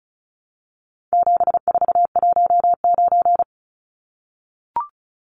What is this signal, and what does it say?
Morse code, heard as a keyed tone, about 34 wpm, 700 Hz. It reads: 7419
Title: Numbers - 4-Digit 35wpm